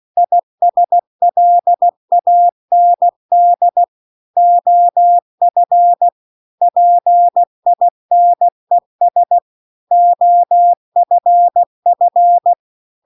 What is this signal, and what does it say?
Morse code, heard as a keyed tone, about 16 wpm, 700 Hz. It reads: ISLAND OF PINES OFF